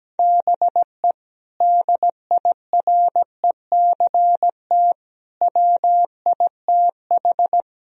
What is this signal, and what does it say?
Morse code, heard as a keyed tone, about 17 wpm, 700 Hz. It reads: BE DIRECT WITH
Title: Sentences from Top 1000 Words - Review Part 2 - 17wpm